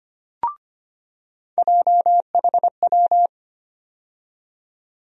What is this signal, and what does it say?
Morse code, heard as a keyed tone, about 25 wpm, 700 Hz. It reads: JHW
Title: USA Call Sign Postfixes - 3 Letters 25wpm